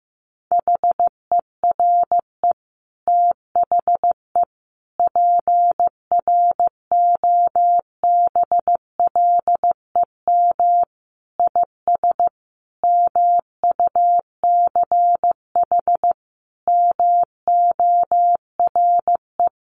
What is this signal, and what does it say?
Morse code, heard as a keyed tone, about 14 wpm, 700 Hz. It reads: HERE THE PROBLEM IS MUCH MORE
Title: Sentences from Top 1200 Words 15wpm